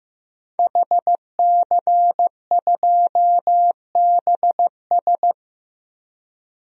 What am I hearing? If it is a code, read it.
Morse code, heard as a keyed tone, about 15 wpm, 700 Hz. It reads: HC2BS
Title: DXCC-100 Call Signs - 5 Characters - 15wpm